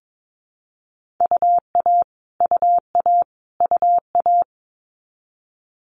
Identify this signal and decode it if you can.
Morse code, heard as a keyed tone, about 22 wpm, 700 Hz. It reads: UA UA UA
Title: Multi-letter Phonemes Repeated 3x 22wpm